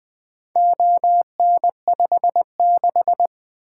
Morse code, heard as a keyed tone, about 21 wpm, 700 Hz. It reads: ON56
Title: Four Letter-Number Combinations - 20wpm